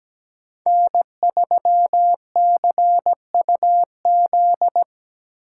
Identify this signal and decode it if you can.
Morse code, heard as a keyed tone, about 20 wpm, 700 Hz. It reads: N3CUZ
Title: Speed Racing - USA Call Signs 17wpm